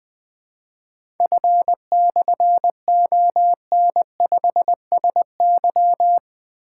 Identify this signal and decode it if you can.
Morse code, heard as a keyed tone, about 20 wpm, 700 Hz. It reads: F/ON5SY
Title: Difficult Call Signs 20wpm